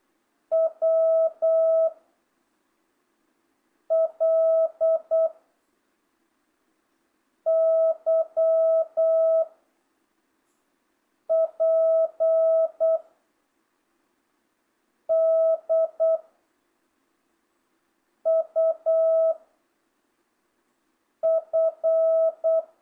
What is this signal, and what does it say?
Morse code, heard as a keyed tone, about 8 wpm, 660 Hz. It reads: W L Y P D U F
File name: jh8cbh / 2025cw